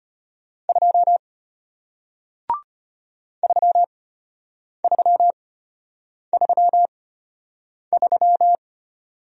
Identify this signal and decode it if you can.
Morse code, heard as a keyed tone, about 33 wpm, spaced as for 8 wpm, 700 Hz. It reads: 2 3333